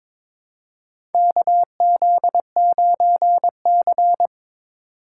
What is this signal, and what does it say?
Morse code, heard as a keyed tone, about 22 wpm, 700 Hz. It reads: KZ9C
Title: USA Call Signs - 4 Characters - 3x Character Spacing - 22wpm